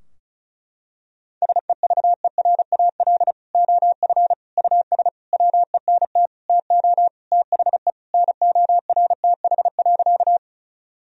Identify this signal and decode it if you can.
Morse code, heard as a keyed tone, about 36 wpm, 700 Hz. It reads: SEVERAL OF US WENT TO THE NORTH.
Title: Sentences from Top 1000 Words - Review Part 2 - 35wpm